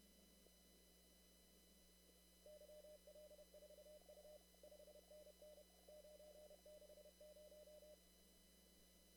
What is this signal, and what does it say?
Morse code, heard as a keyed tone, about 31 wpm, 600 Hz, with noise. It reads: YL4U 5NN 960
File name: CW / kt5j wpx cw-op-k5pi